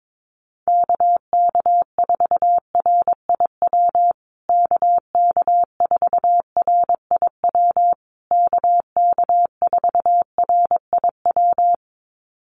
Morse code, heard as a keyed tone, about 22 wpm, 700 Hz. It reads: KK4RIW KK4RIW KK4RIW
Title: USA Call Signs Repeated 3x 22wpm